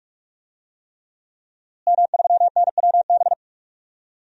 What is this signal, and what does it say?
Morse code, heard as a keyed tone, about 45 wpm, 700 Hz. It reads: M3NWB
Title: DXCC-100 Call Signs - 5 Characters - Repeated 3x - 45wpm